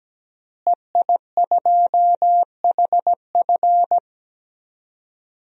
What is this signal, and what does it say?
Morse code, heard as a keyed tone, about 17 wpm, 700 Hz. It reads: EI2HF
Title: DXCC-100 Call Signs - 5 Characters - Repeated 2x - 17wpm